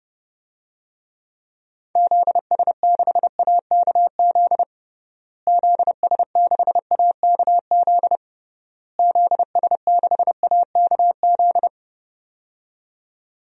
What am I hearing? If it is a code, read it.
Morse code, heard as a keyed tone, about 30 wpm, 700 Hz. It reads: ZS6AKZ ZS6AKZ ZS6AKZ
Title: DXCC-100 Call Signs - 6 Characters - Repeated 3x - 30wpm